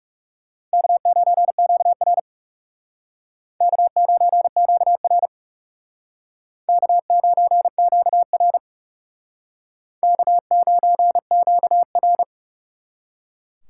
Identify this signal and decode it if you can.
Morse code, heard as a keyed tone, about 36 wpm, 700 Hz. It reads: K9QR K9QR K9QR K9QR
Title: Speed Racing - USA 1x2 Call Signs 30wpm